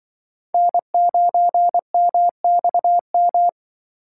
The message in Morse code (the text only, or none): N9MXM